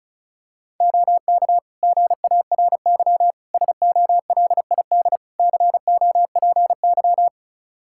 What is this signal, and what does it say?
Morse code, heard as a keyed tone, about 34 wpm, 700 Hz. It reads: OK GARY SOLID COPY
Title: Practice QSO - Exchange 3 - More Info - 35wpm